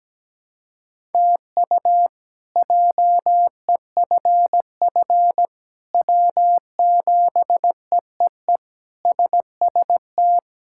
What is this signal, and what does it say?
Morse code, heard as a keyed tone, about 17 wpm, 700 Hz. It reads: TU JEFF W7EEE SST
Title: K1USN SST QSO Practice - 17wpm